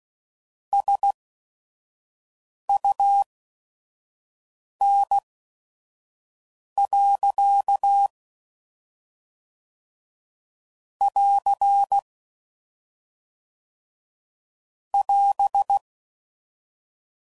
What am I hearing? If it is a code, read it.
Morse code, heard as a keyed tone, about 16 wpm, spaced as for 5 wpm, 800 Hz. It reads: SUN. <AR> <AS>
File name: CW Code Groups @5WPM / Weather Sets / Weather 5 5